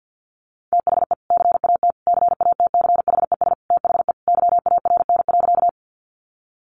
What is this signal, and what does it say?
Morse code, heard as a keyed tone, about 50 wpm, 700 Hz. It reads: THE CAT CATCHES THE YARN.
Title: Additional Easy Phrases 50wpm